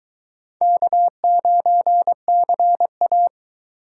Morse code, heard as a keyed tone, about 22 wpm, 700 Hz. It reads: K9CA